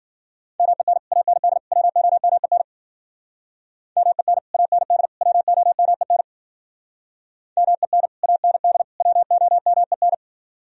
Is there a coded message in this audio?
Yes: Morse code from 0.6 to 10.1 s, heard as a keyed tone, about 53 wpm, 700 Hz. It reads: MEN AND WOMEN MEN AND WOMEN MEN AND WOMEN